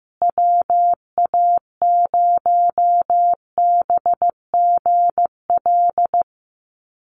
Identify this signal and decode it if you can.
Morse code, heard as a keyed tone, about 15 wpm, 700 Hz. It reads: WA0BGL
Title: USA Call Signs Repeated 3x 15wpm